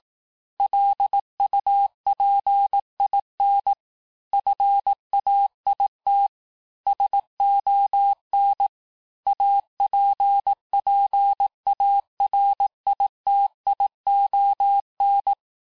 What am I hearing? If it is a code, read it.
Morse code, heard as a keyed tone, about 18 wpm, 800 Hz. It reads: LUPIN FAIT SON APPARITION
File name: cours-du-lundi files / cours-1852-texte-clair-vitesse-17MPM-puis-22MPM